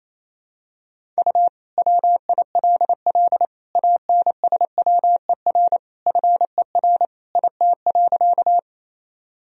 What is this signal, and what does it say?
Morse code, heard as a keyed tone, about 28 wpm, 700 Hz. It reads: U WILL ANSWER FER IT.
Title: Sentences from Top 300 Words and Abbreviations 28wpm